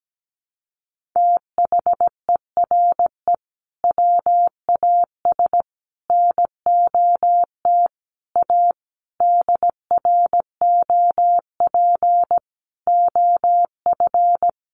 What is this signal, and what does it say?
Morse code, heard as a keyed tone, about 17 wpm, 700 Hz. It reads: THERE WAS NOT A DROP OF
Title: Sentences from Top 1500 Words 17wpm